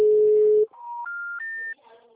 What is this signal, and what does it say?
Special information tone.
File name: out- -2004- - - .210